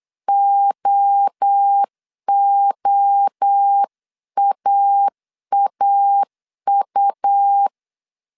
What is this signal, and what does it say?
Morse code, heard as a keyed tone, about 8 wpm, 800 Hz. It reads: OOAAU
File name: CW / DET / 1 3 1500C S C 8wpm 4214